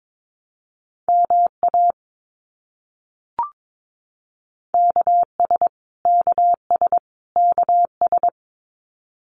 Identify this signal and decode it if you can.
Morse code, heard as a keyed tone, about 22 wpm, 700 Hz. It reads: MA KS KS KS
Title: US States Repeated 3x 22wpm